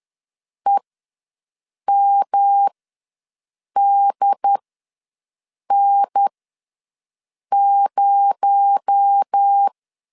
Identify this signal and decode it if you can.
Morse code, heard as a keyed tone, about 10 wpm, spaced as for 6 wpm, 800 Hz. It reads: EMDN0
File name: LEZIONI / 6 PKX0 / 6 PKX0 ws 10wpm 4220